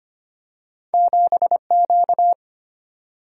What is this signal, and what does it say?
Morse code, heard as a keyed tone, about 25 wpm, 700 Hz. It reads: 7 Q
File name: Two Letter-Number Combinations - 25wpm